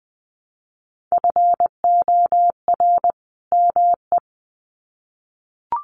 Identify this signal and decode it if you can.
Morse code, heard as a keyed tone, about 20 wpm, 700 Hz. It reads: FOR ME
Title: Sets of 2 Words 20wpm